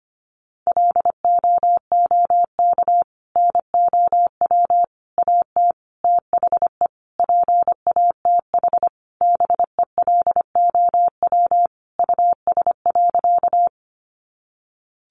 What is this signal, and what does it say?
Morse code, heard as a keyed tone, about 25 wpm, 700 Hz. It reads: LOOK NOW AT THE PATH BELOW US.
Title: Sentences from Top 1400 Words 25wpm